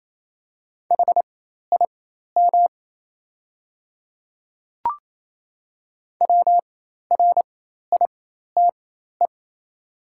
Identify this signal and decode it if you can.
Morse code, heard as a keyed tone, about 28 wpm, spaced as for 13 wpm, 700 Hz. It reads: HIM WRITE
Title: 4x Character Spacing - Top 100 Words - 28wpm